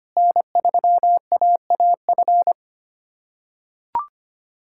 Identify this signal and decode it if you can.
Morse code, heard as a keyed tone, about 25 wpm, 700 Hz. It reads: N3AAF
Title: USA Callsign DE Callsign - 5 Characters - 25wpm